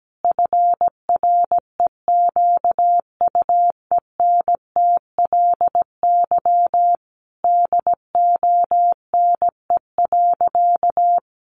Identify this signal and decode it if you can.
Morse code, heard as a keyed tone, about 17 wpm, 700 Hz. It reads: FREQUENTLY DONE.
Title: Sentences from Top 1300 Words 17wpm